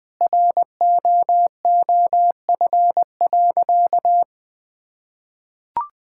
Morse code, heard as a keyed tone, about 21 wpm, 700 Hz. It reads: ROOF.
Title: Sentences with 3-Word Phrases - 20wpm